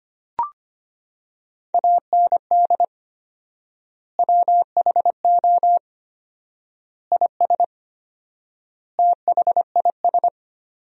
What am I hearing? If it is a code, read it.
Morse code, heard as a keyed tone, about 25 wpm, 700 Hz. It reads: AND WHO IS THIS
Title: 4x Word-Spacing - Sentences from Top 300 Words - 25wpm